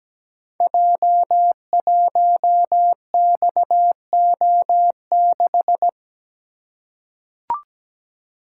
Morse code, heard as a keyed tone, about 17 wpm, 700 Hz. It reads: J1XO6
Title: Code Groups - 5 Letter:Numbers - 17wpm